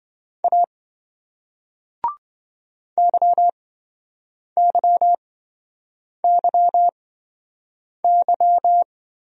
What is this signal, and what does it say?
Morse code, heard as a keyed tone, about 28 wpm, spaced as for 7 wpm, 700 Hz. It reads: A YYYY